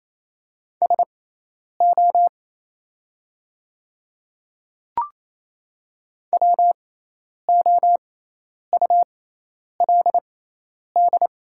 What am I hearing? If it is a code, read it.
Morse code, heard as a keyed tone, about 28 wpm, spaced as for 10 wpm, 700 Hz. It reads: SO WOULD
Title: 6x Character Spacing - Top 100 Words - 28wpm